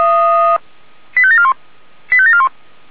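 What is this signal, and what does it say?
Five-tone selective call (ZVEI), heard as quick runs of five tones, tones of about 70 ms. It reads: 85741 85741
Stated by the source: ZVEI five-tone selective call 85753 received by SDR.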